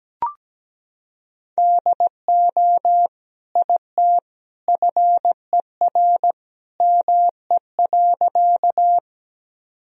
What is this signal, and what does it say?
Morse code, heard as a keyed tone, about 17 wpm, 700 Hz. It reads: DO IT FER ME.
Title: Sentences from Top 200 Words and Abbreviations 17wpm